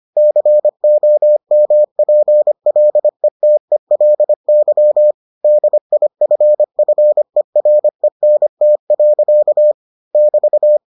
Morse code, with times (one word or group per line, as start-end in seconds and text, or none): 0.2-5.1 COMPLETELY
5.4-9.7 DIFFERENT.
10.2-10.8 <BT>